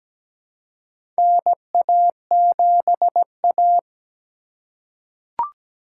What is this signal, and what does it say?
Morse code, heard as a keyed tone, about 17 wpm, 700 Hz. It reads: NA7A
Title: USA Call Signs - 4 Characters - 4x Character Spacing - 17wpm